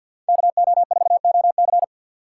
Morse code, heard as a keyed tone, about 53 wpm, 690 Hz. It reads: KO4OC
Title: USA Call Signs - 5 Characters - 6x Character Spacing - 50wpm